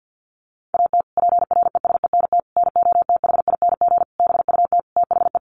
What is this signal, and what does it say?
Morse code, heard as a keyed tone, about 50 wpm, 700 Hz. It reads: AT PRESENT NOTHING BUT THE
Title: Sentences from Top 1500 Words 50wpm